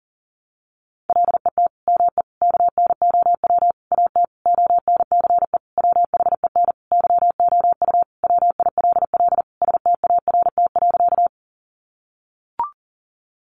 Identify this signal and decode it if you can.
Morse code, heard as a keyed tone, about 40 wpm, 700 Hz. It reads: LET ME KNOW AT ONCE WHEN YOU WILL START.
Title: Sentences from Top 300 Words 40wpm